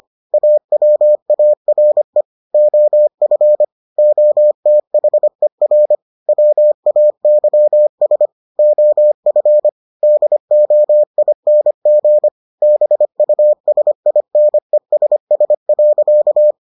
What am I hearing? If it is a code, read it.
Morse code, heard as a keyed tone, about 25 wpm, 600 Hz. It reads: AWARE OF OTHER WAYS OF DOING BUSINESS.